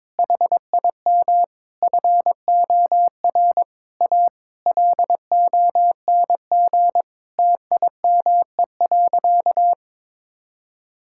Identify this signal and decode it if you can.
Morse code, heard as a keyed tone, about 22 wpm, 700 Hz. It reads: HIM FOR A LONG TIME.
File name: practice / 22wpm / Sentences with 4-Word Phrases - 22wpm